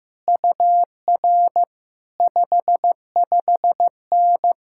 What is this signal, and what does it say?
Morse code, heard as a keyed tone, about 15 wpm, 700 Hz. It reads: UR 55N
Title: Speed-Racing Common QSO Phrases Encore - 15wpm